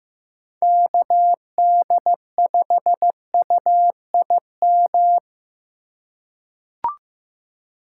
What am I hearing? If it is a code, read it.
Morse code, heard as a keyed tone, about 15 wpm, 700 Hz. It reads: KD5UIM
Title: USA Call Signs - 2x Character Spacing - 15wpm